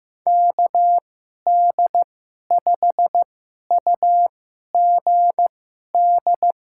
Morse code, heard as a keyed tone, about 16 wpm, spaced as for 11 wpm, 700 Hz. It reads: KD5UGD
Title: USA Call Signs - 6 Characters - 2x Character Spacing - 15wpm